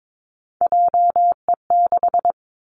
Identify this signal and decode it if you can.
Morse code, heard as a keyed tone, about 22 wpm, 700 Hz. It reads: JE6
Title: Three Challenging Letter-Number Combinations 22wpm